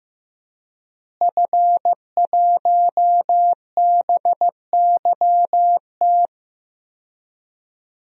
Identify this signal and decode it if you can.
Morse code, heard as a keyed tone, about 15 wpm, 700 Hz. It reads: F1BYT
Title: DXCC-100 Call Signs - 5 Characters - Repeated 3x - 15wpm